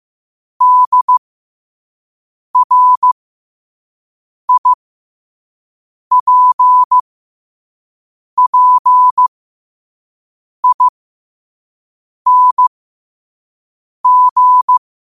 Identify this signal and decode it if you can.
Morse code, heard as a keyed tone, about 15 wpm, spaced as for 5 wpm, 1000 Hz. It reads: DRIPPING
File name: qotdfiles / 5wpm